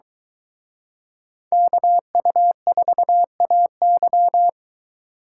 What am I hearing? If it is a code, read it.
Morse code, heard as a keyed tone, about 22 wpm, 700 Hz. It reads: KU4AY